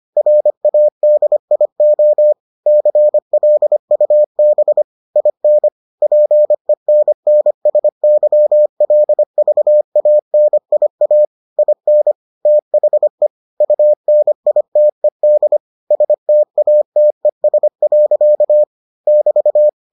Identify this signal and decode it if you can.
Morse code, heard as a keyed tone, about 25 wpm, 600 Hz. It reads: RADIO CLUB IN PENNSYLVANIA IN THE UNITED STATES. <BT>